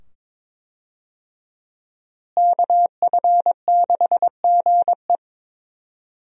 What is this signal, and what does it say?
Morse code, heard as a keyed tone, about 22 wpm, 700 Hz. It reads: KF6GE
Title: USA Call Signs - 5 Characters - Repeated 2x - 22wpm